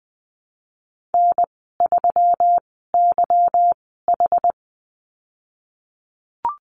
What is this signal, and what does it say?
Morse code, heard as a keyed tone, about 20 wpm, spaced as for 15 wpm, 700 Hz. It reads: N3YH
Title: 167 Lesson 27g - Number 7 - Call Signs 20wpm